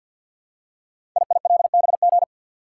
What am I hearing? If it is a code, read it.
Morse code, heard as a keyed tone, about 50 wpm, 700 Hz. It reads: IILBG
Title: Code Groups - 5 Letters - 50wpm